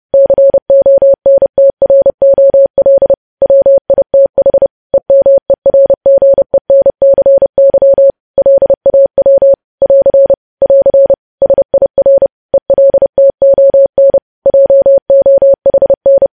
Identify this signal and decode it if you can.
Morse code, heard as a keyed tone, about 30 wpm, 550 Hz. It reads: CONTROL WITH EMERGENCY LAW <AR> <AR> SIR ELTON JOHN
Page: news headlines in morse code at 30 wpm